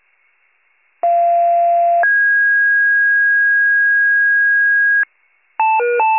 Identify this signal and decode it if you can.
Two-tone page, heard as a one-second tone, then a three-second tone, about 690 then 1650 Hz. It is heard from 1.0 to 5.0 s.